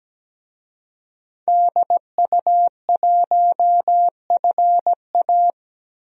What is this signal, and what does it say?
Morse code, heard as a keyed tone, about 17 wpm, 700 Hz. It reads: DU1FA
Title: DXCC-100 Call Signs - 5 Characters - 17wpm